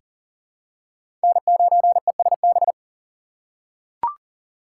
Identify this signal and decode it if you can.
Morse code, heard as a keyed tone, about 40 wpm, 700 Hz. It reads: N9ESB
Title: USA Call Signs - 5 Characters - 3x Character Spacing - 40wpm